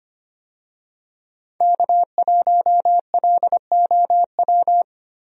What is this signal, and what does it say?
Morse code, heard as a keyed tone, about 25 wpm, 700 Hz. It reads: K1LOW
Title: USA Call Signs - 5 Characters - Repeated 3x - 25wpm